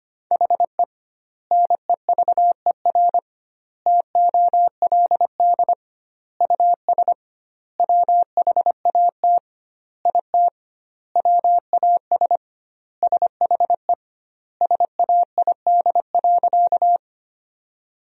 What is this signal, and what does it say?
Morse code, heard as a keyed tone, about 25 wpm, 700 Hz. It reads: HE NEVER TOLD US WHAT IT WAS SHE SAID.
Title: 2x Word-Spacing - Sentences from Top 400 Words - 25wpm